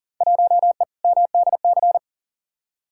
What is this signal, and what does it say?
Morse code, heard as a keyed tone, about 40 wpm, 700 Hz. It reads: 1E MDC
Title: Field Day Exchanges - 40wpm